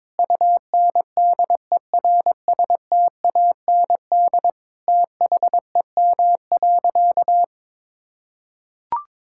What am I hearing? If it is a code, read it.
Morse code, heard as a keyed tone, about 22 wpm, 700 Hz. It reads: UNDERSTAND THEM.
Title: Sentences from Top 1000 Words - Review Part 5 - 22wpm